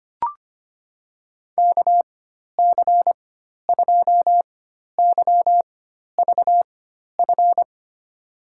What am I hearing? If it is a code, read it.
Morse code, heard as a keyed tone, about 25 wpm, spaced as for 12 wpm, 700 Hz. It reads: KC2YVF
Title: USA Call Signs - 6 Characters - 4x Character Spacing - 25wpm